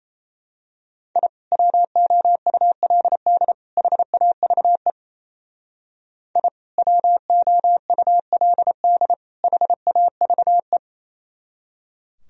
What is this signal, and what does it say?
Morse code, heard as a keyed tone, about 31 wpm, 700 Hz. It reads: I WOULD HAVE I WOULD HAVE
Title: Speed-Racing - Sets of 3 Words 28wpm